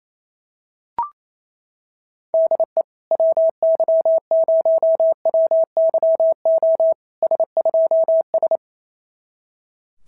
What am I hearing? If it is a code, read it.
Morse code, heard as a keyed tone, about 28 wpm, 650 Hz. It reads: DE WY0WYO S2S
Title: SOTA QSO Practice 28wpm